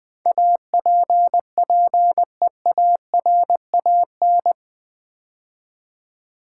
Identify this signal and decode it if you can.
Morse code, heard as a keyed tone, about 20 wpm, 700 Hz. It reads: APPEARAN
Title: Build A Word - 4 Letters At A Time - 10-Letter Words - 20wpm